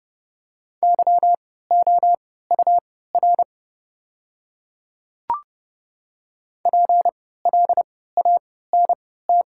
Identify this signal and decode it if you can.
Morse code, heard as a keyed tone, about 30 wpm, spaced as for 17 wpm, 700 Hz. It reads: YOUR PLANT
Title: 3x Character Spacing - Top 500 Words - 30wpm